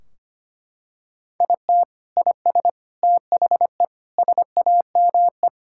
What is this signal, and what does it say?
Morse code, heard as a keyed tone, about 25 wpm, 700 Hz. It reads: IT IS THE SAME
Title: Sentences from Top 1600 Words 25wpm